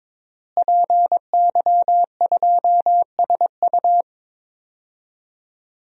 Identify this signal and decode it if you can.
Morse code, heard as a keyed tone, about 22 wpm, 700 Hz. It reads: PY2SU
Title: DXCC-100 Call Signs - 5 Characters - Repeated 2x - 22wpm